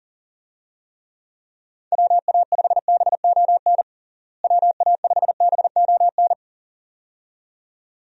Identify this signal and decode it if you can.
Morse code, heard as a keyed tone, about 37 wpm, 700 Hz. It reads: WA5BON WA5BON
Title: USA Call Signs - 6 Characters - Repeated 2x - 40wpm